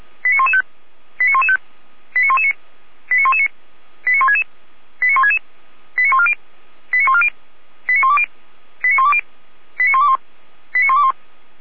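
Five-tone selective call (ZVEI), heard as quick runs of five tones, tones of about 80 ms. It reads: 89115 89115 89119 89119 89166 89166 89140 89140 89120 89120 89121 89121
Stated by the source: ZVEI five-tone selective call 89097 received by SDR.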